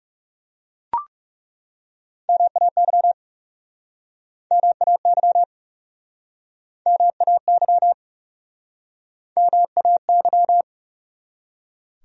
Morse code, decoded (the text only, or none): MAY MAY MAY MAY